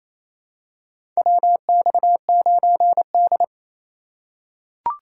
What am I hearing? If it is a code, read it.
Morse code, heard as a keyed tone, about 28 wpm, 700 Hz. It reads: WX9D
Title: USA Call Signs - 4 Characters - 8x Character Spacing - 28wpm